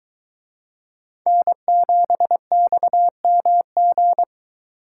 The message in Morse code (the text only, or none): N7XMG